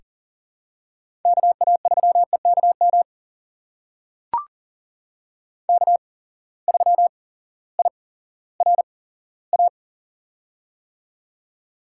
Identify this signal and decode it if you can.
Morse code, heard as a keyed tone, about 40 wpm, spaced as for 18 wpm, 700 Hz. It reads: KA3EKM K3IRA
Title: USA Call Signs - 8x Character Spacing - 40wpm